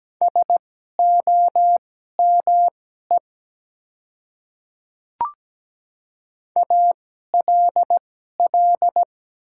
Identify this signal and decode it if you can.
Morse code, heard as a keyed tone, about 17 wpm, spaced as for 12 wpm, 700 Hz. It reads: SOME ALL
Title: 2x Character Spacing - Top 100 Words - 17wpm